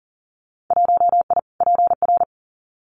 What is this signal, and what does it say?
Morse code, heard as a keyed tone, about 40 wpm, 700 Hz. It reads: 1I PR
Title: Winter Field Day Exchanges Encore 40wpm